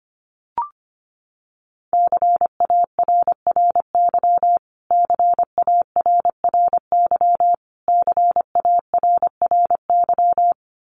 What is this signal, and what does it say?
Morse code, heard as a keyed tone, about 25 wpm, 700 Hz. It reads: CARRY CARRY CARRY
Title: Top 300 Words Repeated 25wpm